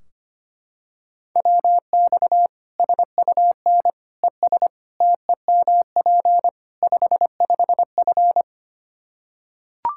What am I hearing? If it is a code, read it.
Morse code, heard as a keyed tone, about 25 wpm, 700 Hz. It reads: WX SUN ES TEMP 55F
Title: K1USN SST QSO Practice 25wpm